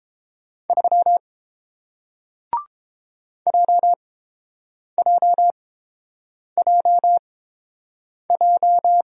Morse code, heard as a keyed tone, about 30 wpm, spaced as for 7 wpm, 700 Hz. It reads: 3 JJJJ